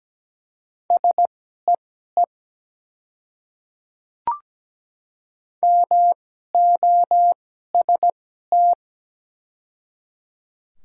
Morse code, heard as a keyed tone, about 17 wpm, spaced as for 12 wpm, 700 Hz. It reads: SEE MOST
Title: 2x Character Spacing - Top 200 Words - 17wpm